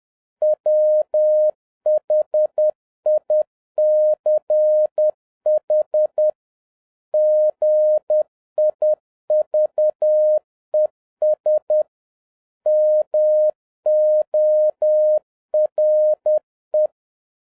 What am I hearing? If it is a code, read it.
Morse code, heard as a keyed tone, about 10 wpm, 610 Hz. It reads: WHICH GIVES MORE
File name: 2011-07-mp3 / n21 GREETINGS FROM SINGAPORE 10wpm